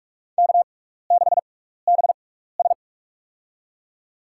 Morse code, heard as a keyed tone, about 46 wpm, spaced as for 15 wpm, 700 Hz. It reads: K6BS
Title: USA Call Signs - 4 Characters - 6x Character Spacing - 45wpm